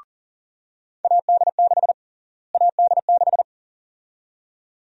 Morse code, heard as a keyed tone, about 40 wpm, 700 Hz. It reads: AD6 AD6
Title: USA Call Sign Prefixes - Repeated 2x - 40wpm